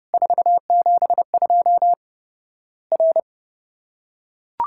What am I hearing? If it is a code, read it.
Morse code, heard as a keyed tone, about 30 wpm, 700 Hz. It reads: 472 R
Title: Worked All Europe DX Contest - QTC Exchange - 30wpm